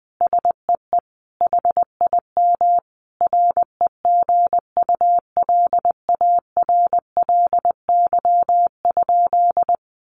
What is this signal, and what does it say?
Morse code, heard as a keyed tone, about 19 wpm, 700 Hz. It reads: SEE HIM REGULARLY?
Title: Sentences from Top 1400 Words 20wpm